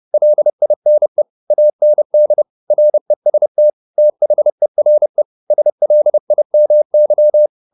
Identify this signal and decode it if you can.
Morse code, heard as a keyed tone, about 30 wpm, 600 Hz. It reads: LINE AND REST THERE SLIMY
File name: the old man and the sea / the old man and the sea 1 30wpm